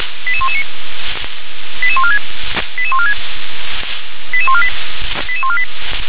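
Five-tone selective call (ZVEI), heard as quick runs of five tones, tones of about 70 ms. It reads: 88119 88146 88146 88148 88148
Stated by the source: ZVEI five-tone call received by SDR.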